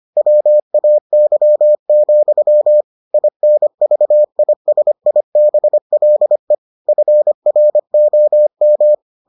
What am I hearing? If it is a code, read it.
Morse code, heard as a keyed tone, about 25 wpm, 600 Hz. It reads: WAY, INVISIBLE FROM